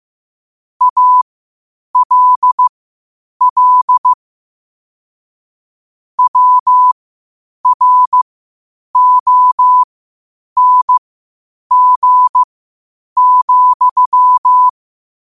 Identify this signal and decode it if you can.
Morse code, heard as a keyed tone, about 15 wpm, spaced as for 8 wpm, 1000 Hz. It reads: ALL WRONG,
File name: qotdfiles / 8wpm